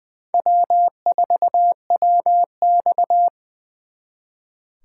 Morse code, heard as a keyed tone, about 23 wpm, 700 Hz. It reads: W4WX
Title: Speed Racing - USA 1x2 Call Signs 20wpm